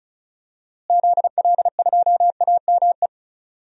Morse code, heard as a keyed tone, about 34 wpm, 700 Hz. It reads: ZL2AME
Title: DXCC-100 Call Signs - 6 Characters - Repeated 2x - 35wpm